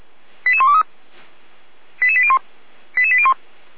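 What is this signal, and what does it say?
Five-tone selective call (ZVEI), heard as quick runs of five tones, tones of about 70 ms. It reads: 88123 88801 88801
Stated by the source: Five-tone ZVEI selective call received by SDR.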